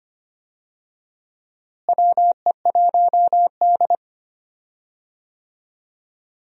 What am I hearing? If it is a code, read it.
Morse code, heard as a keyed tone, about 25 wpm, 700 Hz. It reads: WE1D
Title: USA Call Signs - 4 Characters - Repeated 3x - 25wpm